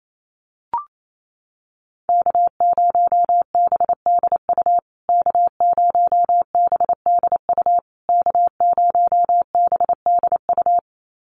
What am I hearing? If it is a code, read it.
Morse code, heard as a keyed tone, about 28 wpm, 700 Hz. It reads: K0BDU K0BDU K0BDU
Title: USA Call Signs Repeated 3x 28wpm